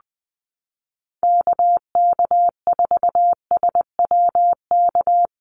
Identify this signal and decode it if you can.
Morse code, heard as a keyed tone, about 20 wpm, 700 Hz. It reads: KK4SWK
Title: USA Call Signs Repeated 2x 20wpm